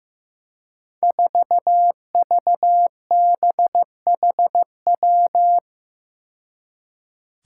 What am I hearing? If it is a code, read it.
Morse code, heard as a keyed tone, about 15 wpm, 700 Hz. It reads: 4VBHW
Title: Five Challenging Letter-Number Combinations 15wpm